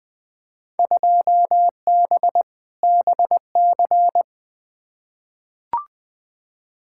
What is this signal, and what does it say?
Morse code, heard as a keyed tone, about 20 wpm, 700 Hz. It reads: 2B BC
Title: Field Day Exchanges Encore - 20wpm